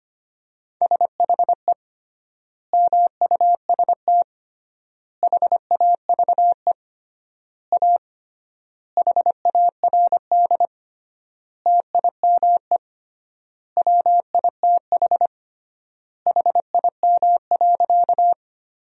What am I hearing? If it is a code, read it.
Morse code, heard as a keyed tone, about 25 wpm, 700 Hz. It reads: SHE MUST HAVE A HARD TIME WITH HIM.
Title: 3x Word-Spacing - Sentences from Top 300 Words - 25wpm